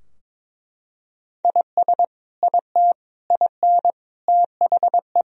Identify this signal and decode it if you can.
Morse code, heard as a keyed tone, about 22 wpm, 700 Hz. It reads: IS IT IN THE
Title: Sentences from Top 1000 Words - Review Part 4 - 22wpm